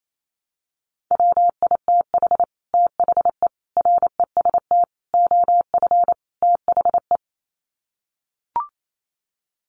Morse code, heard as a keyed tone, about 28 wpm, 700 Hz. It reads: WITH THE REST OF THE
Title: Sets of 5 Words 28wpm